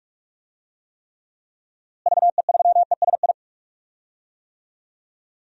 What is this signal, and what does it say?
Morse code, heard as a keyed tone, about 46 wpm, 700 Hz. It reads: VE3ESI
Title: DXCC-100 Call Signs - 6 Characters - Repeated 2x - 45wpm